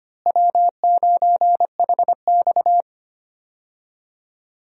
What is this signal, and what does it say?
Morse code, heard as a keyed tone, about 25 wpm, 700 Hz. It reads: W9HX
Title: USA Call Signs - 4 Characters - Repeated 2x - 25wpm